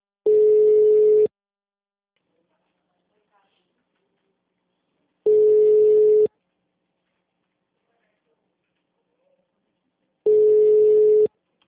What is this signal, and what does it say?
Ringback tone (Europe, 425 Hz), with speech.
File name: out- -2017- - - .59805